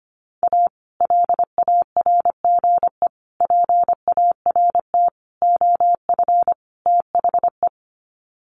Morse code, heard as a keyed tone, about 25 wpm, 700 Hz. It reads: A LARGE PART OF THE
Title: Sets of 5 Words Encore2 25wpm